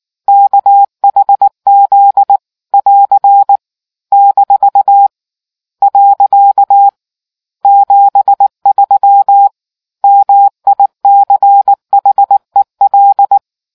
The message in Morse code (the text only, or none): KHZ <AR> - . 73 MICHEL